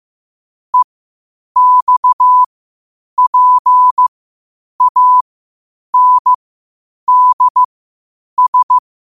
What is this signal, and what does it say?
Morse code, heard as a keyed tone, about 15 wpm, spaced as for 8 wpm, 1000 Hz. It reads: EXPANDS
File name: qotdfiles / 8wpm